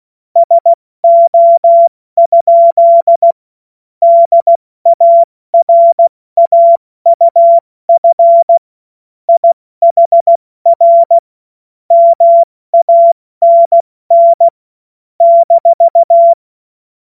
Morse code, heard as a keyed tone, about 16 wpm, 680 Hz. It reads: SO? DARAUF IHR MANN -
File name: Morsekurs / Witze / Witz 09 April 2024 70 F80